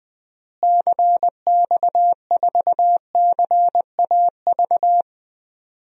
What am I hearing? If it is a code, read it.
Morse code, heard as a keyed tone, about 20 wpm, 700 Hz. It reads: CX4CAV
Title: DXCC-100 Call Signs - 6 Characters - Repeated 3x - 20wpm